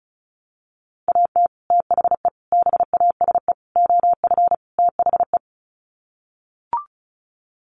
Morse code, heard as a keyed tone, about 34 wpm, 700 Hz. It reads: AT THE BASE OF THE
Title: Sets of 5 Words 35wpm